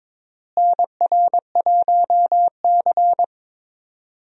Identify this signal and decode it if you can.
Morse code, heard as a keyed tone, about 22 wpm, 700 Hz. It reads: NR1C
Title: USA Call Signs - 4 Characters - Repeated 3x - 22wpm